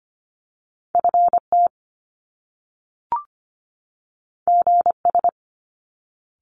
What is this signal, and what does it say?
Morse code, heard as a keyed tone, about 25 wpm, 700 Hz. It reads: FT GS
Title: Top Two Letter Combinations in Words 25wpm